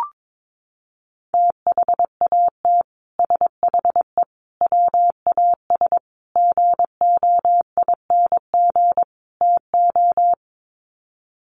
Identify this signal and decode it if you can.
Morse code, heard as a keyed tone, about 22 wpm, 700 Hz. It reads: THAT SHE WAS GOING TO
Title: Sets of 5 Words Encore2 22wpm